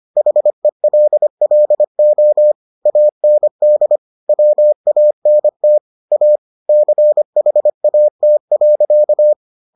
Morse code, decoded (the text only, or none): HELLO AND WANT A CHAT.